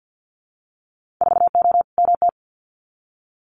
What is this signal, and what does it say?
Morse code, heard as a keyed tone, about 50 wpm, 700 Hz. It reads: 4O MT
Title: Winter Field Day Exchanges Encore 50wpm